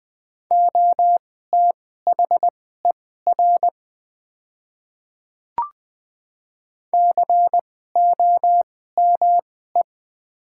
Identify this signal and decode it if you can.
Morse code, heard as a keyed tone, about 20 wpm, spaced as for 14 wpm, 700 Hz. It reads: OTHER COME
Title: 2x Character Spacing - Top 100 Words - 20wpm